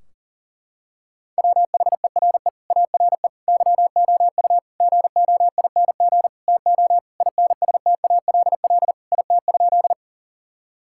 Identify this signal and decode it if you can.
Morse code, heard as a keyed tone, about 40 wpm, 700 Hz. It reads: WHERE ARE YOU GOING TO INSTALL IT?
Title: Sentences from Top 1600 Words 40wpm